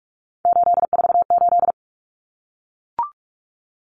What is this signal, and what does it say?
Morse code, heard as a keyed tone, about 44 wpm, 700 Hz. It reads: 848
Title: Common Cut Numbers - 3-Digit 45wpm